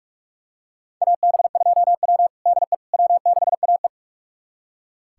Morse code, heard as a keyed tone, about 42 wpm, 700 Hz. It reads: AB2W DE W6AE
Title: USA Callsign DE Callsign - 4 Characters - 45wpm